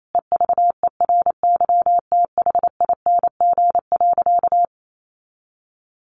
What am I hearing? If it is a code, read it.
Morse code, heard as a keyed tone, about 28 wpm, 700 Hz. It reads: EVERYTHING.
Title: Sentences from Top 700 Words 28wpm